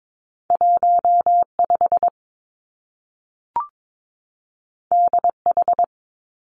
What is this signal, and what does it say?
Morse code, heard as a keyed tone, about 22 wpm, 700 Hz. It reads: 1 5 D H
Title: Two Challenging Letter-Number Combinations 22wpm